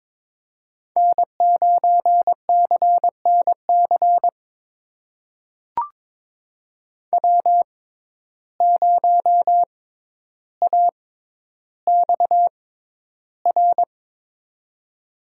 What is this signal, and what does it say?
Morse code, heard as a keyed tone, about 21 wpm, spaced as for 11 wpm, 700 Hz. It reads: N9CNC W0AXR
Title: USA Call Signs - 5 Characters - 6x Character Spacing - 22wpm